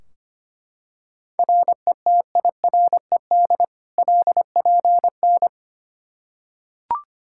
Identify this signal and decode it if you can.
Morse code, heard as a keyed tone, about 25 wpm, 700 Hz. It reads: RETIRED LPN
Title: Practice QSO - Exchange 3 - More Info - 25wpm